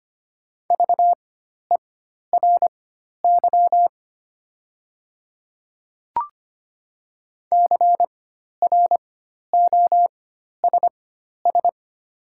Morse code, heard as a keyed tone, about 25 wpm, spaced as for 12 wpm, 700 Hz. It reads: VERY CROSS
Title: 4x Character Spacing - Top 500 Words - 25wpm